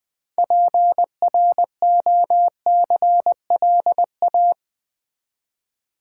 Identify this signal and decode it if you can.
Morse code, heard as a keyed tone, about 20 wpm, 700 Hz. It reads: PROCLA
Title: Build A Word - 2 Letters At A Time - 8-Letter Words - 20wpm